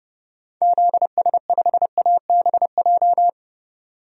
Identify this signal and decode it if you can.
Morse code, heard as a keyed tone, about 30 wpm, 700 Hz. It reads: ZS5ABJ
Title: DXCC-100 Call Signs - 6 Characters - 30wpm